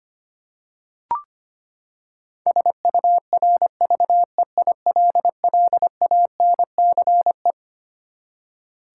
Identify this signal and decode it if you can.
Morse code, heard as a keyed tone, about 25 wpm, 700 Hz. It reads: SURVEILLANCE
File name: Twelve Letter Words - 25wpm